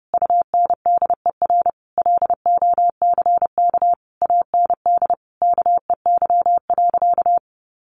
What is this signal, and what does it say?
Morse code, heard as a keyed tone, about 30 wpm, 700 Hz. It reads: UNDER LOCK AND KEY.
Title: Sentences from Top 1400 Words 30wpm